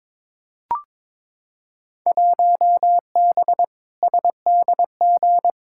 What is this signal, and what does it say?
Morse code, heard as a keyed tone, about 22 wpm, 700 Hz. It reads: 1B SDG
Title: Field Day Exchanges - 22wpm